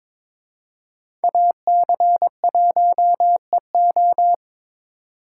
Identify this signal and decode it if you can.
Morse code, heard as a keyed tone, about 26 wpm, 700 Hz. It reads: AC1EO
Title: Speed Racing - USA 2x2 Call Signs 22wpm